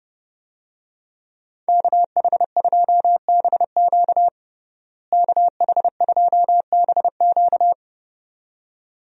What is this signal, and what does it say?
Morse code, heard as a keyed tone, about 30 wpm, 700 Hz. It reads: KH2BQ KH2BQ
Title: USA Call Signs - 5 Characters - Repeated 2x - 30wpm